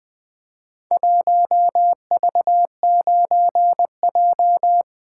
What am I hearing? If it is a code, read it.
Morse code, heard as a keyed tone, about 20 wpm, 700 Hz. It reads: 1V9J
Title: Four Letter-Number Combinations - 20wpm